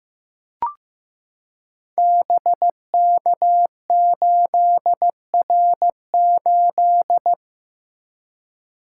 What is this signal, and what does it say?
Morse code, heard as a keyed tone, about 15 wpm, 700 Hz. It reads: BK8R8
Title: Code Groups - 5 Letter:Numbers - 15wpm